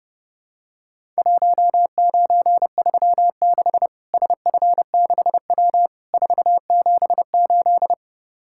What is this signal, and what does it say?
Morse code, heard as a keyed tone, about 30 wpm, 700 Hz. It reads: 1936 SF6W 478
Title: Worked All Europe DX Contest - QTC Exchange - 30wpm